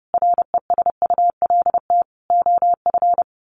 Morse code, heard as a keyed tone, about 30 wpm, 700 Hz. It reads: RESULT OF
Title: Sets of 3 Words 30wpm